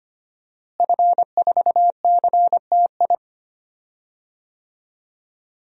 Morse code, heard as a keyed tone, about 25 wpm, 700 Hz. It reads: F4CTI
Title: DXCC-100 Call Signs - 5 Characters - Repeated 3x - 25wpm